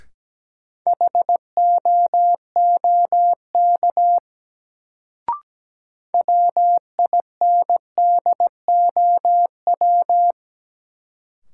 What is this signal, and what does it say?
Morse code, heard as a keyed tone, about 17 wpm, 700 Hz. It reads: HOOK WINDOW
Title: Sentences from Top 1600 Words - New Word Drill - 17wpm